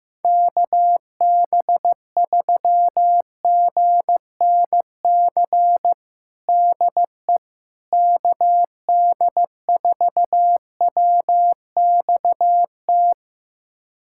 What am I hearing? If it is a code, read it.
Morse code, heard as a keyed tone, about 15 wpm, 700 Hz. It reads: KB3GNC DE KD4WXT
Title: USA Callsign DE Callsign - 6 Characters - 15wpm